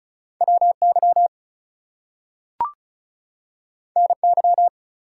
Morse code, heard as a keyed tone, about 36 wpm, 700 Hz. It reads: WY NY
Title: US States and Canadian Provinces 35wpm